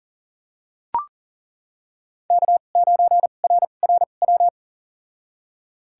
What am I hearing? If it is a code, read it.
Morse code, heard as a keyed tone, about 37 wpm, spaced as for 28 wpm, 700 Hz. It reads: K9RRW
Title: USA Call Signs - 5 Characters - 2x Character Spacing - 40wpm